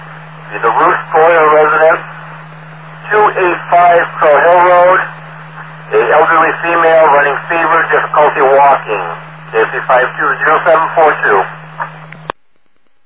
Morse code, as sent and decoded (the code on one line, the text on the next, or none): none